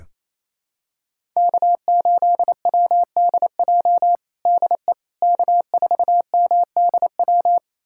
Morse code, heard as a keyed tone, about 28 wpm, 700 Hz. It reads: K8WDJ DE K4MDW
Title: USA Callsign DE Callsign - 5 Characters - 28wpm